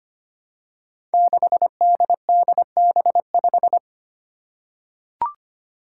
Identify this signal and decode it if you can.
Morse code, heard as a keyed tone, about 25 wpm, 700 Hz. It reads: 6DDB5
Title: Five Challenging Letter-Number Combinations 25wpm